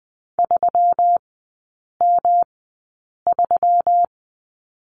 Morse code, heard as a keyed tone, about 20 wpm, spaced as for 8 wpm, 700 Hz. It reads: 3M3
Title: 121 Lesson 20e - Number 3 - Three Characters 20wpm